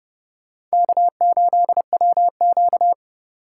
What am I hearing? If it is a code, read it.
Morse code, heard as a keyed tone, about 34 wpm, 700 Hz. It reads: K8WQ
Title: Speed Racing - USA Call Signs 30wpm